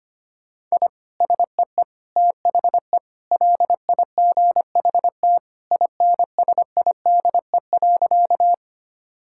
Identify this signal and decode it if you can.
Morse code, heard as a keyed tone, about 25 wpm, 700 Hz. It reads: I SEE THE LIGHT INSIDE.
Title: Sentences from Top 1000 Words - Review Part 6 - 25wpm